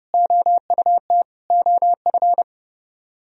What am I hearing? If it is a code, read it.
Morse code, heard as a keyed tone, about 37 wpm, 700 Hz. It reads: OUT OF
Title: Speed-Racing - Sets of 4 Words 30wpm